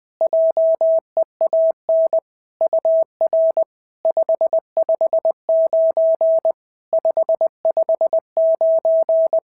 Morse code, heard as a keyed tone, about 21 wpm, 650 Hz. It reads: JEAN UR 559 559
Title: SOTA QSO Practice 20wpm